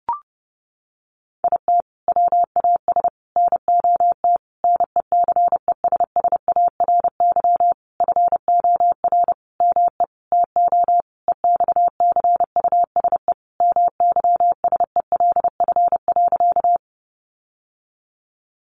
Morse code, heard as a keyed tone, about 30 wpm, 700 Hz. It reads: IT WAS NOT NECESSARY FOR ME TO EXCUSE MYSELF.
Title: Sentences from Top 1300 Words 30wpm